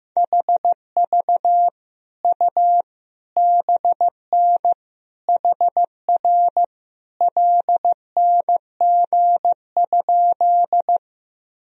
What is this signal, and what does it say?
Morse code, heard as a keyed tone, about 15 wpm, 700 Hz. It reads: HV U BN HR LNG?
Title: Sentences from Top 200 Words and Abbreviations 15wpm